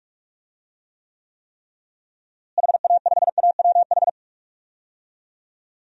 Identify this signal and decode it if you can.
Morse code, heard as a keyed tone, about 41 wpm, 700 Hz. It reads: HA5AWH
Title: DXCC-100 Call Signs - 6 Characters - Repeated 3x - 45wpm